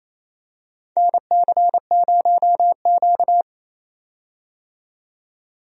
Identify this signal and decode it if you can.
Morse code, heard as a keyed tone, about 28 wpm, 700 Hz. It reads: NC0Q
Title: USA Call Signs - 4 Characters - 28wpm